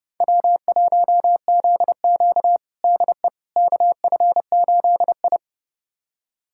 Morse code, heard as a keyed tone, about 32 wpm, 700 Hz. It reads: W1ZQ DE KF8I
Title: USA Callsign DE Callsign - 4 Characters - 30wpm